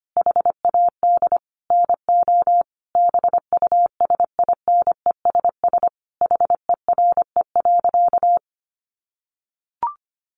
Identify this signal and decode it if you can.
Morse code, heard as a keyed tone, about 25 wpm, 700 Hz. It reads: HAD NO BUSINESS HERE.
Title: Sentences from Top 1300 Words 25wpm